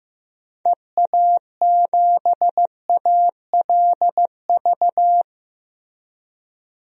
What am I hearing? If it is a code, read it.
Morse code, heard as a keyed tone, about 15 wpm, 700 Hz. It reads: EA7ALV
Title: DXCC-100 Call Signs - 6 Characters - Repeated 3x - 15wpm